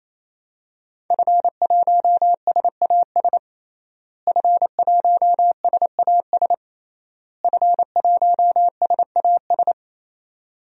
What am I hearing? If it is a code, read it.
Morse code, heard as a keyed tone, about 28 wpm, 700 Hz. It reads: F1SAS F1SAS F1SAS
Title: DXCC-100 Call Signs - 5 Characters - Repeated 3x - 28wpm